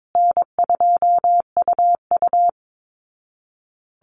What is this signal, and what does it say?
Morse code, heard as a keyed tone, about 22 wpm, 700 Hz. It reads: N2UU
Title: Easy Call Signs Repeated 3x - 22wpm